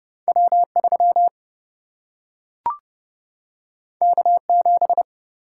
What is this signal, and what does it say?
Morse code, heard as a keyed tone, about 30 wpm, 700 Hz. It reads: W3 K7
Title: Single-Letter Single-Number 30wpm